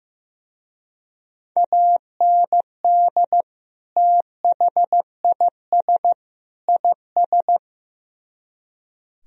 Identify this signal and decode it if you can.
Morse code, heard as a keyed tone, about 16 wpm, 700 Hz. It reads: AND THIS IS
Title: Speed-Racing - Sets of 3 Words 15wpm